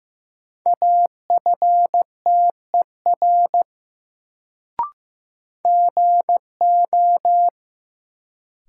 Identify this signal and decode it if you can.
Morse code, heard as a keyed tone, about 15 wpm, 700 Hz. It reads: AFTER GO
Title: Sets of 2 Words - New Word Drill - 15wpm